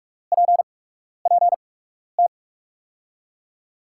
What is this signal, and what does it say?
Morse code, heard as a keyed tone, about 42 wpm, spaced as for 12 wpm, 700 Hz. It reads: PPT
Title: USA Call Signs - 6 Characters - 8x Character Spacing - 45wpm